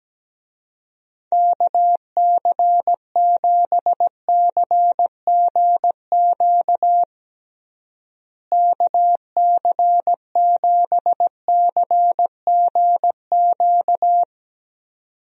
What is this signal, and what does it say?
Morse code, heard as a keyed tone, about 17 wpm, 700 Hz. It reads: KC7CGQ KC7CGQ
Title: USA Call Signs - 6 Characters - Repeated 2x - 17wpm